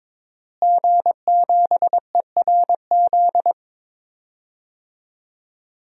Morse code, heard as a keyed tone, about 22 wpm, 700 Hz. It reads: G7ERZ
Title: DXCC-100 Call Signs - 5 Characters - Repeated 3x - 22wpm